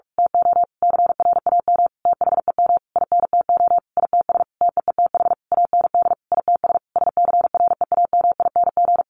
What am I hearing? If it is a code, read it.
Morse code, heard as a keyed tone, about 44 wpm, 700 Hz. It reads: TO CRAM THEM INTO ITS TEETH AND ITS SCREAMING
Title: A Sound Of Thunder 45wpm